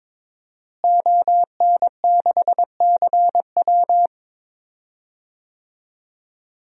Morse code, heard as a keyed tone, about 21 wpm, 700 Hz. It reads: ON6CW
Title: DXCC-100 Call Signs - 5 Characters - Repeated 3x - 22wpm